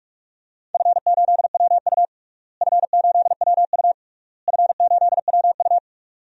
Morse code, heard as a keyed tone, about 42 wpm, 700 Hz. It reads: F8WU F8WU F8WU
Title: DXCC-100 Call Signs - 4 Characters - Repeated 3x - 45wpm